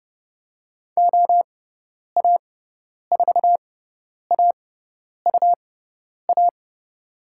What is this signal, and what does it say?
Morse code, heard as a keyed tone, about 30 wpm, spaced as for 10 wpm, 700 Hz. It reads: OA4AUA
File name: Lesson 18g - Letter m - Call Signs 30wpm